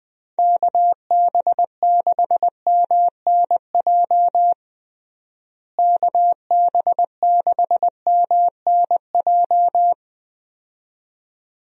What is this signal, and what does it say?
Morse code, heard as a keyed tone, about 20 wpm, 700 Hz. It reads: KB6MNJ KB6MNJ
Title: USA Call Signs - 6 Characters - Repeated 2x - 20wpm